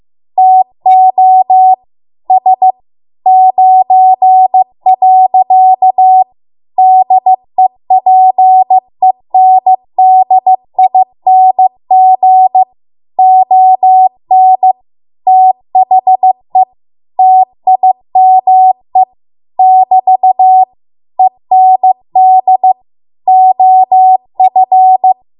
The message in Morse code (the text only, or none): TO S 9. DEPENDING ON THE TIME <BT> END OF